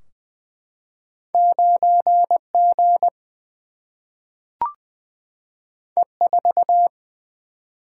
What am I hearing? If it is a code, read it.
Morse code, heard as a keyed tone, about 20 wpm, 700 Hz. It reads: 9 G E 4
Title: Single-Letter Single-Number 20wpm